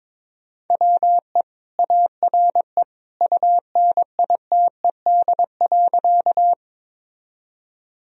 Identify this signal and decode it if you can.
Morse code, heard as a keyed tone, about 21 wpm, 700 Hz. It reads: WE ARE UNITED.
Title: Sentences from Top 1000 Words - Review Part 4 - 22wpm